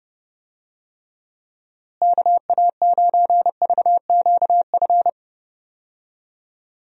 Morse code, heard as a keyed tone, about 30 wpm, 700 Hz. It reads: KA9VQF
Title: USA Call Signs - 6 Characters - Repeated 2x - 30wpm